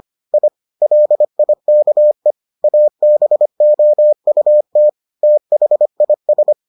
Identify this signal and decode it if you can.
Morse code, heard as a keyed tone, about 25 wpm, 600 Hz. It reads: I LIKE ABOUT THIS